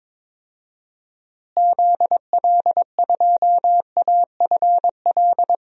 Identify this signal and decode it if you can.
Morse code, heard as a keyed tone, about 22 wpm, 700 Hz. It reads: ZL2AFL
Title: DXCC-100 Call Signs - 6 Characters - Repeated 2x - 22wpm